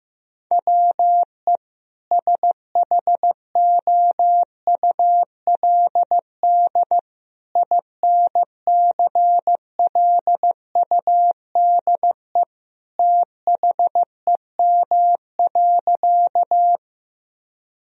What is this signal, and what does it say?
Morse code, heard as a keyed tone, about 16 wpm, 700 Hz. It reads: WE SHOULD INCLUDE THEM.
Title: Sentences from Top 1000 Words - Review Part 3 - 15wpm